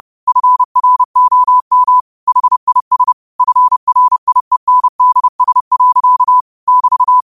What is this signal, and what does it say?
Morse code, heard as a keyed tone, about 30 wpm, 1000 Hz. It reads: FROM HIS FRIENDS. <BT>